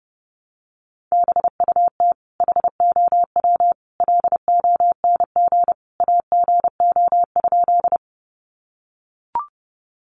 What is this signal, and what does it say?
Morse code, heard as a keyed tone, about 30 wpm, 700 Hz. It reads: BUT HOW LONG AGO?
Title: Sentences from Top 500 Words 30wpm